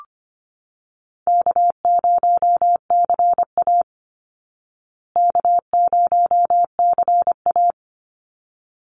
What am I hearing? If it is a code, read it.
Morse code, heard as a keyed tone, about 25 wpm, 700 Hz. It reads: K0CA K0CA
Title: Easy Call Signs Repeated 2x - 25wpm